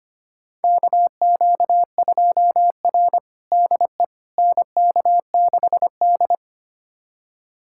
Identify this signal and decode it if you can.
Morse code, heard as a keyed tone, about 25 wpm, 700 Hz. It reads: KQ2R DE NK6D
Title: USA Callsign DE Callsign - 4 Characters - 25wpm